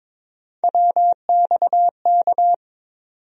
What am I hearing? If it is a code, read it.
Morse code, heard as a keyed tone, about 22 wpm, 700 Hz. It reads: WXK
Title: USA Call Sign Postfixes - 3 Letters 22wpm